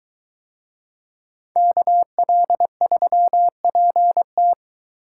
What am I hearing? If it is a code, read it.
Morse code, heard as a keyed tone, about 22 wpm, 700 Hz. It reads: KL3PT